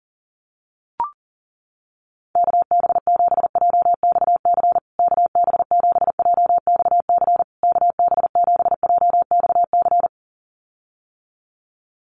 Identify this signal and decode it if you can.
Morse code, heard as a keyed tone, about 40 wpm, 700 Hz. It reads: KB7JXC KB7JXC KB7JXC
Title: USA Call Signs Repeated 3x 40wpm